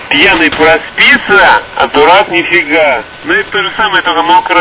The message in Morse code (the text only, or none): none